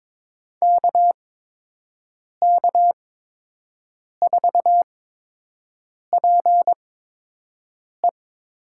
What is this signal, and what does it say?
Morse code, heard as a keyed tone, about 22 wpm, spaced as for 6 wpm, 700 Hz. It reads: KK4PE
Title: USA Call Signs - 8x Character Spacing - 22wpm